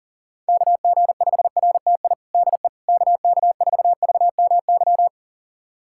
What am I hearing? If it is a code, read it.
Morse code, heard as a keyed tone, about 40 wpm, 700 Hz. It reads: KG5RTI DE KK4VMY
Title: USA Callsign DE Callsign 40wpm